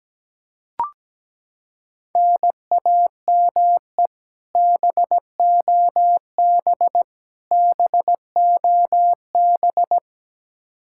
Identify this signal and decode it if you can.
Morse code, heard as a keyed tone, about 17 wpm, 700 Hz. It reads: NAME BOB BOB
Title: Practice QSO - Exchange 1 - Beginning - 17wpm